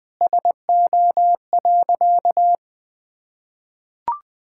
Morse code, heard as a keyed tone, about 21 wpm, 700 Hz. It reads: SO.
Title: Sentences with 2-Word Phrases - Spoken Beginning - 20wpm